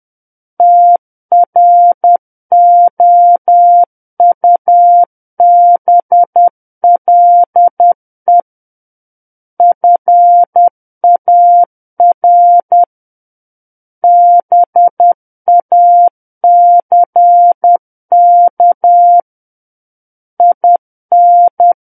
Morse code, heard as a keyed tone, about 10 wpm, 700 Hz. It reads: TROUBLE FAR BACK IN